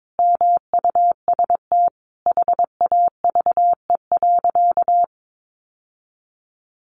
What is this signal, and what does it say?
Morse code, heard as a keyed tone, about 22 wpm, 700 Hz. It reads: MUST HAVE.
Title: Sentences from Top 1100 Words 22wpm